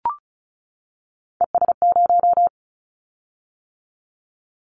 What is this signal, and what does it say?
Morse code, heard as a keyed tone, about 34 wpm, 700 Hz. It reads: ES0
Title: Callsign Prefixes for Top 100 DXCC - 35wpm